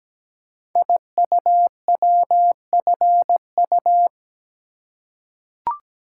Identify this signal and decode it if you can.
Morse code, heard as a keyed tone, about 17 wpm, 700 Hz. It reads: IUWFU
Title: Code Groups - 5 Letters - 17wpm